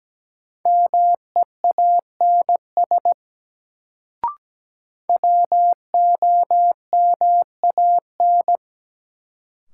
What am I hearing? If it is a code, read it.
Morse code, heard as a keyed tone, about 17 wpm, 700 Hz. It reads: MEANS WOMAN
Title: Sets of 5 Words - New Word Drill - 17wpm